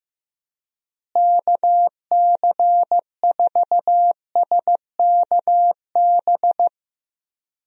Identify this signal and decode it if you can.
Morse code, heard as a keyed tone, about 15 wpm, 700 Hz. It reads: KC4SKB
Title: USA Call Signs - 6 Characters - Repeated 2x - 15wpm